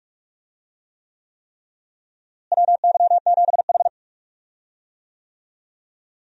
Morse code, heard as a keyed tone, about 46 wpm, 700 Hz. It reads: WY7H
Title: USA Call Signs - 4 Characters - Repeated 3x - 45wpm